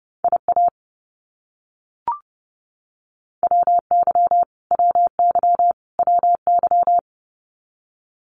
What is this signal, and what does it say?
Morse code, heard as a keyed tone, about 30 wpm, 700 Hz. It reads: IA WY WY WY
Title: US States Repeated 3x 30wpm